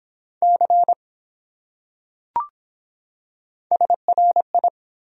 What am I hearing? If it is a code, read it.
Morse code, heard as a keyed tone, about 23 wpm, 700 Hz. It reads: C SRI